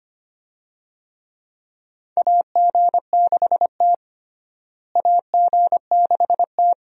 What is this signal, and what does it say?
Morse code, heard as a keyed tone, about 25 wpm, 700 Hz. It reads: AG6T AG6T
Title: USA Call Signs - 4 Characters - Repeated 2x - 25wpm